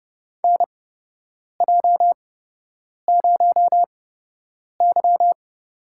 Morse code, heard as a keyed tone, about 30 wpm, spaced as for 8 wpm, 700 Hz. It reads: NJ0Y
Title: USA Call Signs - 4 Characters - 8x Character Spacing - 30wpm